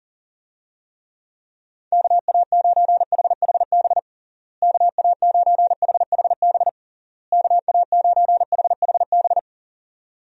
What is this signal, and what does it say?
Morse code, heard as a keyed tone, about 40 wpm, 700 Hz. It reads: KA9HHB KA9HHB KA9HHB
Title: USA Call Signs - 6 Characters - Repeated 3x - 40wpm